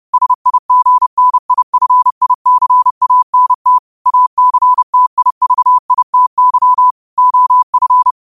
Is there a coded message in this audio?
Yes: Morse code from 0.1 to 8.1 s, heard as a keyed tone, about 30 wpm, 1000 Hz. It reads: SIGNIFICANT ACTIVITY OF